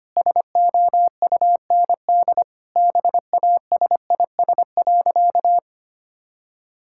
Morse code, heard as a keyed tone, about 25 wpm, 700 Hz. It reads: SOUND BASIS.
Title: Sentences from Top 1000 Words - Review Part 5 - 25wpm